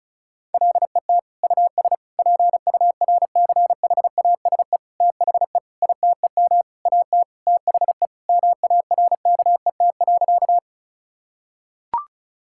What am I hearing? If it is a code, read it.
Morse code, heard as a keyed tone, about 36 wpm, 700 Hz. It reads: LET US PURCHASE THE ITEM AT THE MARKET.
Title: Sentences from Top 1000 Words - Review Part 6 - 35wpm